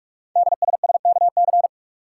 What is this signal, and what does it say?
Morse code, heard as a keyed tone, about 44 wpm, 700 Hz. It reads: DSSKC
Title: Code Groups - 5 Letters - 45wpm